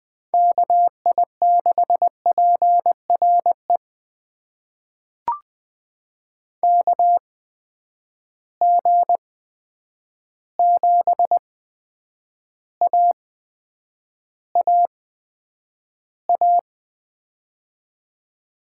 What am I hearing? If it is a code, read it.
Morse code, heard as a keyed tone, about 20 wpm, spaced as for 9 wpm, 700 Hz. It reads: KI6PRE KG7AAA
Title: USA Call Signs - 6 Characters - 8x Character Spacing - 20wpm